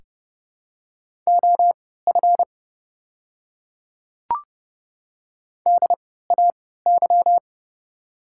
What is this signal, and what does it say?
Morse code, heard as a keyed tone, about 30 wpm, spaced as for 17 wpm, 700 Hz. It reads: OF DAY
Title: 3x Character Spacing - Top 100 Words - 30wpm